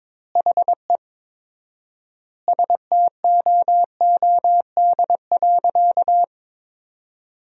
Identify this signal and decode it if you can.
Morse code, heard as a keyed tone, about 22 wpm, 700 Hz. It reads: HE STOOD.
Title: 4x Word-Spacing - Sentences from Top 500 Words - 22wpm